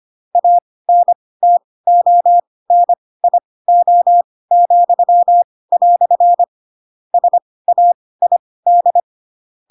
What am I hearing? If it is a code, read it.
Morse code, heard as a keyed tone, about 25 wpm, spaced as for 18 wpm, 700 Hz. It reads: ANTONIO," SAID
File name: Pinocchio / Pinocchio ch2 12 25 ik1hge